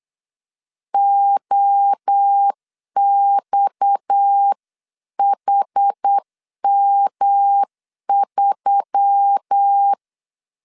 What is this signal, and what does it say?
Morse code, heard as a keyed tone, about 8 wpm, 800 Hz. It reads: OXHM3
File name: CW / DET / 1 9 1500C S A 8wpm 5637